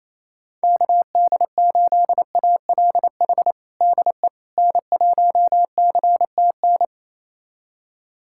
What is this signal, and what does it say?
Morse code, heard as a keyed tone, about 28 wpm, 700 Hz. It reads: KD8ALH DE N1CTN
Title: USA Callsign DE Callsign 28wpm